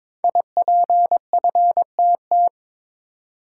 Morse code, heard as a keyed tone, about 23 wpm, 700 Hz. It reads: IPFTT
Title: Code Groups - 5 Letters - 22wpm